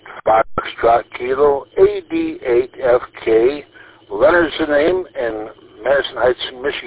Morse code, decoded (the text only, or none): none